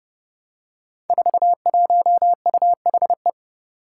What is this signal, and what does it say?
Morse code, heard as a keyed tone, about 30 wpm, 700 Hz. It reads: 41UHE
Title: Five Challenging Letter-Number Combinations 30wpm